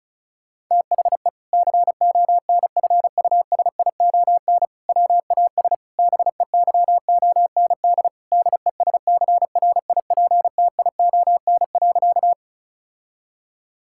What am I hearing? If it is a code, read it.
Morse code, heard as a keyed tone, about 34 wpm, 700 Hz. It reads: THE CONFUSION WAS BEYOND DESCRIPTION.
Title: Sentences from Top 1600 Words 35wpm